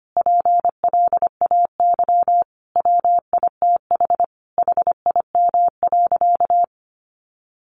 Morse code, heard as a keyed tone, about 25 wpm, 700 Hz. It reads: PLAY WITH HIM.
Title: Sentences from Top 1500 Words 25wpm